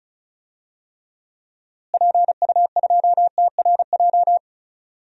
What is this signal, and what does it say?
Morse code, heard as a keyed tone, about 34 wpm, 700 Hz. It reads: PU2TRJ
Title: DXCC-100 Call Signs - 6 Characters - Repeated 3x - 35wpm